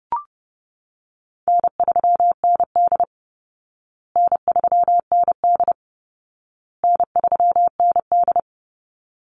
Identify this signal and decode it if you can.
Morse code, heard as a keyed tone, about 30 wpm, 700 Hz. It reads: N3ND N3ND N3ND
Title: Easy Call Signs Repeated 3x - 30wpm